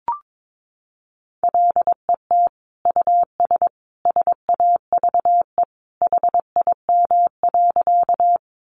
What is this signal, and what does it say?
Morse code, heard as a keyed tone, about 22 wpm, 700 Hz. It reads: LET US SAVE HIM.
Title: Sentences from Top 1100 Words 22wpm